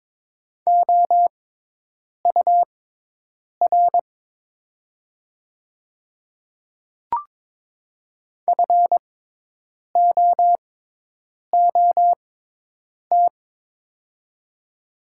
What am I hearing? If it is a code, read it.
Morse code, heard as a keyed tone, about 22 wpm, spaced as for 8 wpm, 700 Hz. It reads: OUR FOOT
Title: 6x Character Spacing - Top 500 Words - 22wpm